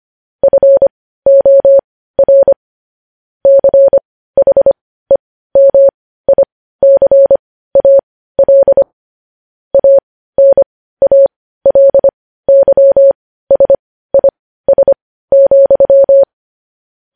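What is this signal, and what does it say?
Morse code, heard as a keyed tone, about 25 wpm, spaced as for 15 wpm, 550 Hz. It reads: FOR CHEMICAL ANALYSIS,
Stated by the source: Morse news headlines practice at 15.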